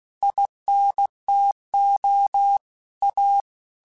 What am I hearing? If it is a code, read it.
Morse code, heard as a keyed tone, about 16 wpm, 800 Hz. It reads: INTO A
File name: QSOs / QSOs-mp3s / WA7QOX 2@15